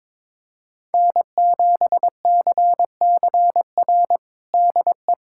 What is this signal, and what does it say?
Morse code, heard as a keyed tone, about 22 wpm, 700 Hz. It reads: N7CCR DE
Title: USA Callsign DE Callsign - 5 Characters - 22wpm